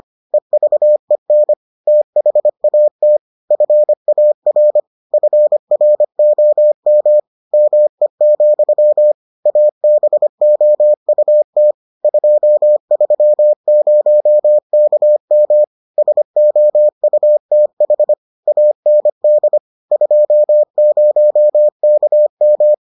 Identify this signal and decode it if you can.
Morse code, heard as a keyed tone, about 25 wpm, 600 Hz. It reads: EVEN THAT FAR FROM ME, ABOUT 230KM SOUTH AND 20KM